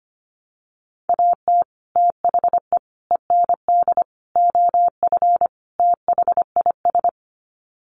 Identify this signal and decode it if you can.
Morse code, heard as a keyed tone, about 25 wpm, 700 Hz. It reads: AT THE END OF THIS
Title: Sets of 5 Words 25wpm